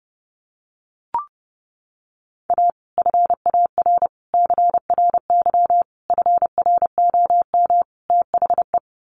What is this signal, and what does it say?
Morse code, heard as a keyed tone, about 30 wpm, 700 Hz. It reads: A FAR CRY FROM THE
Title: Sets of 5 Words Encore 30wpm